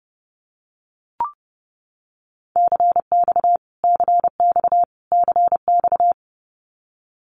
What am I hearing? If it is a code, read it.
Morse code, heard as a keyed tone, about 30 wpm, 700 Hz. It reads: CX CX CX
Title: Abbreviations Repeated 3x - 30wpm